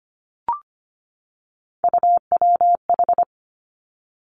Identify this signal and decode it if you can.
Morse code, heard as a keyed tone, about 25 wpm, 700 Hz. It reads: UWH
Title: Three Challenging Letter-Number Combinations 25wpm